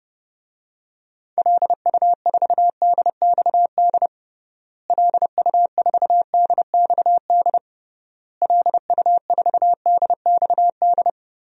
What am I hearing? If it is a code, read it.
Morse code, heard as a keyed tone, about 30 wpm, 700 Hz. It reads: LU4DXD LU4DXD LU4DXD
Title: DXCC-100 Call Signs - 6 Characters - Repeated 3x - 30wpm